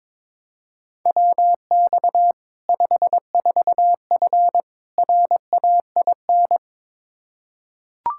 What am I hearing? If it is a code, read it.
Morse code, heard as a keyed tone, about 22 wpm, 700 Hz. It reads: WX 54F RAIN
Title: Practice QSO - Exchange 2 - Info - 22wpm